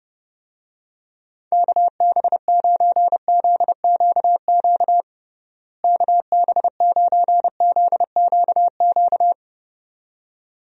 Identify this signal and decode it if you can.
Morse code, heard as a keyed tone, about 29 wpm, 700 Hz. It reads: KB9ZQQ KB9ZQQ
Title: USA Call Signs - 6 Characters - Repeated 2x - 30wpm